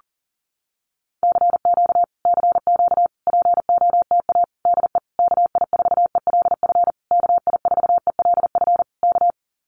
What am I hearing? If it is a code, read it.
Morse code, heard as a keyed tone, about 40 wpm, 700 Hz. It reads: CQ CQ POTA DE KI4ELF KI4ELF K
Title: Common QSO Phrases Encore 40wpm